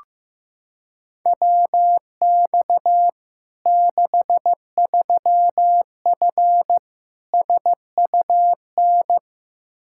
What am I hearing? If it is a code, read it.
Morse code, heard as a keyed tone, about 16 wpm, 700 Hz. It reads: WX 63F SUN
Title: Practice QSO - Exchange 2 - Info - 15wpm